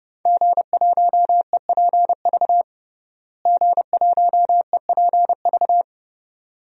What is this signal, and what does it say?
Morse code, heard as a keyed tone, about 30 wpm, 700 Hz. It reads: G1EPV G1EPV
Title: DXCC-100 Call Signs - 5 Characters - Repeated 2x - 30wpm